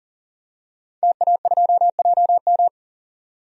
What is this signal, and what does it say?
Morse code, heard as a keyed tone, about 40 wpm, 700 Hz. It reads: TA2JM
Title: DXCC-100 Call Signs - 5 Characters - 40wpm